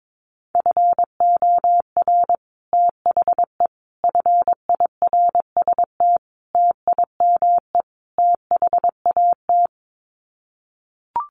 Morse code, heard as a keyed tone, about 22 wpm, 700 Hz. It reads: FOR THE FIRST TIME THAT
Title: Sets of 5 Words Encore 22wpm